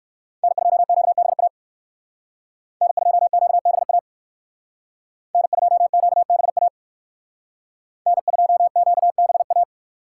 Morse code, heard as a keyed tone, about 56 wpm, 700 Hz. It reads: N2QBA N2QBA N2QBA N2QBA
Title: Speed Racing - USA Call Signs 45wpm